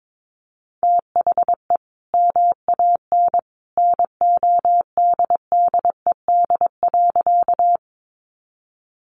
Morse code, heard as a keyed tone, about 22 wpm, 700 Hz. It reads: THE MAN NODDED.
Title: A Sound Of Thunder 22wpm